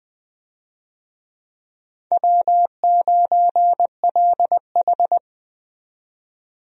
Morse code, heard as a keyed tone, about 21 wpm, 700 Hz. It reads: W9LH
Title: USA Call Signs - 4 Characters - 20wpm